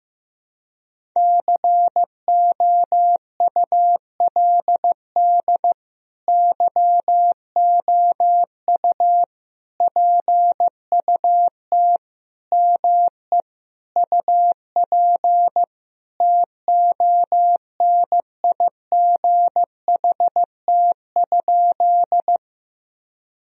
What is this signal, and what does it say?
Morse code, heard as a keyed tone, about 15 wpm, 700 Hz. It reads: COULD YOU PUT ME UP TONIGHT?
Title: Sentences from Top 1000 Words - Review Part 3 - 15wpm